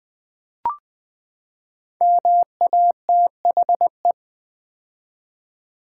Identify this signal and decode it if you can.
Morse code, heard as a keyed tone, about 20 wpm, 700 Hz. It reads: MATHE
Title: Build A Word - 5 Letters At A Time - 12-Letter Words - 20wpm